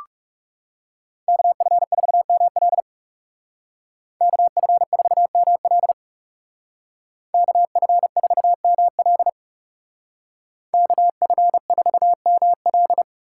Morse code, heard as a keyed tone, about 36 wpm, 700 Hz. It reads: KF4ML KF4ML KF4ML KF4ML
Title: Speed Racing - USA 2x2 Call Signs 30wpm